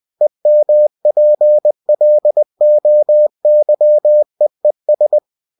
Morse code, heard as a keyed tone, about 20 wpm, 600 Hz. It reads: EMPLOYEES